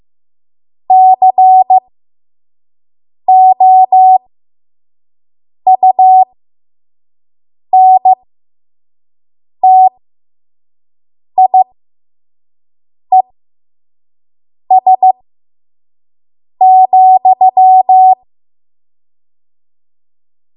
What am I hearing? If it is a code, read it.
Morse code, heard as a keyed tone, about 15 wpm, spaced as for 5 wpm, 750 Hz. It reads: COUNTIES,
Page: 5 WPM Code Archive